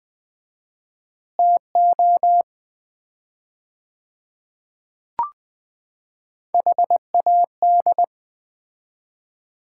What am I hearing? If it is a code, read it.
Morse code, heard as a keyed tone, about 20 wpm, 700 Hz. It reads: TO HAD
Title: 4x Word-Spacing - Sets of 5 Words - 20wpm